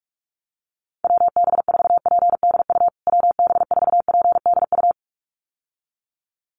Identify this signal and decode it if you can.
Morse code, heard as a keyed tone, about 44 wpm, 700 Hz. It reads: WB4PDU WB4PDU
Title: USA Call Signs Repeated 2x 45wpm